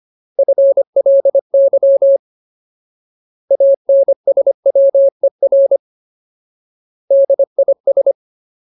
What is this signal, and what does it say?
Morse code, heard as a keyed tone, about 25 wpm, 550 Hz. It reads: FLY ANSWER DIS
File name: Ultimate Word List / 25WPM Ultimate